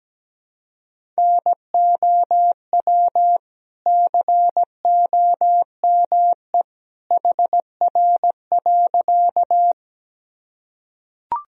Morse code, heard as a keyed tone, about 17 wpm, 700 Hz. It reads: NOW COME HR.
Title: Sentences from Top 200 Words and Abbreviations 17wpm